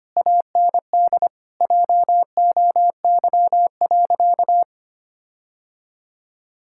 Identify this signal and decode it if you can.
Morse code, heard as a keyed tone, about 25 wpm, 700 Hz. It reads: AND JOY.
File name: practice / 25wpm / Difficult Sentences with Hints - 25wpm